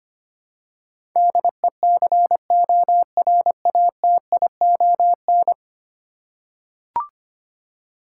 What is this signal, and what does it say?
Morse code, heard as a keyed tone, about 25 wpm, 700 Hz. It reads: DECORATION
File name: 10 Letter Words - First 3 Letters Given - 25wpm